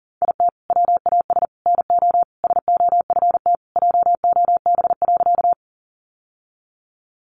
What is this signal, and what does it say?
Morse code, heard as a keyed tone, about 40 wpm, 700 Hz. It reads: IT WAS NO SOFT JOB.
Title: Sentences from Top 1500 Words 40wpm